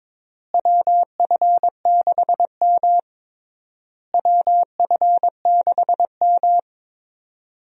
Morse code, heard as a keyed tone, about 22 wpm, 700 Hz. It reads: WF6M WF6M
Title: USA Call Signs - 4 Characters - Repeated 3x - 22wpm